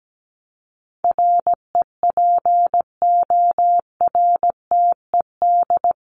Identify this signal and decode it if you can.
Morse code, heard as a keyed tone, about 17 wpm, 700 Hz. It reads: REPORTED
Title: ED Suffix - Top 100 Words with Base Words 17wpm